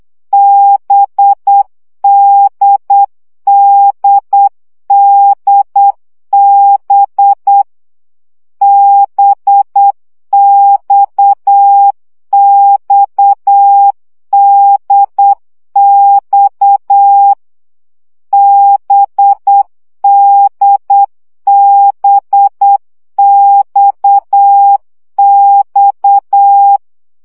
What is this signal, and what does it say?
Morse code, heard as a keyed tone, about 8 wpm, 800 Hz. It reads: BDDDB BXXDX BDBXX
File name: CW gyakorlas d b x 7 baud